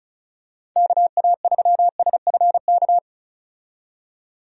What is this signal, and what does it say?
Morse code, heard as a keyed tone, about 34 wpm, 700 Hz. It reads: KA3SFK
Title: USA Call Signs - 6 Characters - 35wpm